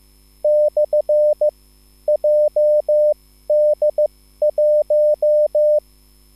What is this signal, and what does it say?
Morse code, heard as a keyed tone, about 15 wpm, spaced as for 13 wpm, 600 Hz. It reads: / JD1